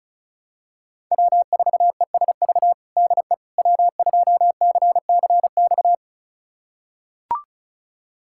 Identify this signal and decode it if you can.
Morse code, heard as a keyed tone, about 36 wpm, 700 Hz. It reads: W4ESV DE W2CCX
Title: USA Callsign DE Callsign - 5 Characters - 35wpm